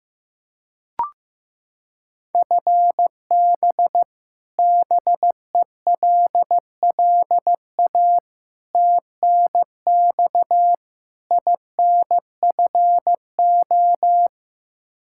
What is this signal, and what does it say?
Morse code, heard as a keyed tone, about 16 wpm, 700 Hz. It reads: FB BELLA TNX INFO
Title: Practice QSO - Exchange 3 - More Info - 15wpm